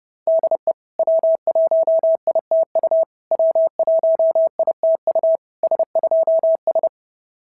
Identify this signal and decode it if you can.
Morse code, heard as a keyed tone, about 28 wpm, 660 Hz. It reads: DE W1ITU W1ITU S2S
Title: SOTA QSO Practice 30wpm